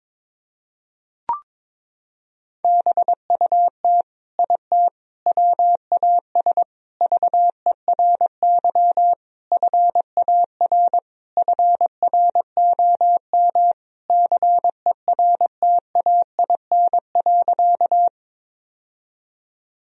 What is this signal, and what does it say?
Morse code, heard as a keyed tone, about 22 wpm, 700 Hz. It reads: BUT IT WAS VERY FAR FROM CERTAIN.
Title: Sentences from Top 1000 Words - Review Part 2 - 22wpm